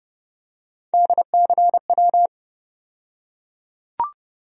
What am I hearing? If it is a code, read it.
Morse code, heard as a keyed tone, about 30 wpm, 700 Hz. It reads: DCW
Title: USA Call Sign Postfixes - 3 Letters 30wpm